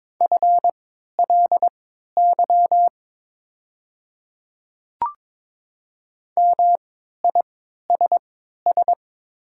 Morse code, heard as a keyed tone, about 23 wpm, spaced as for 13 wpm, 690 Hz. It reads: FLY MISS
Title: 3x Character Spacing - Top 500 Words - 22wpm